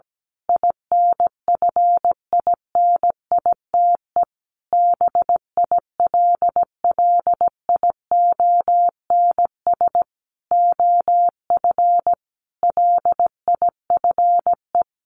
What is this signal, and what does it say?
Morse code, heard as a keyed tone, about 17 wpm, 700 Hz. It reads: INFINITE BILLIONS OF LIFE
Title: A Sound Of Thunder 17wpm